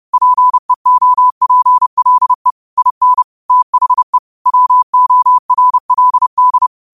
Morse code, heard as a keyed tone, about 30 wpm, 1000 Hz. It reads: PEOPLE IN THE WORLD
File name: qotdfiles / 30wpm